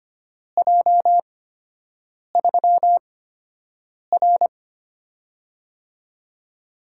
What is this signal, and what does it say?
Morse code, heard as a keyed tone, about 25 wpm, spaced as for 7 wpm, 700 Hz. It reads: J3R
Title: USA Call Signs - 4 Characters - 8x Character Spacing - 25wpm